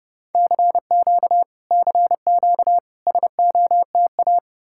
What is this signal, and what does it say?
Morse code, heard as a keyed tone, about 34 wpm, 700 Hz. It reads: CQ CQ SOTA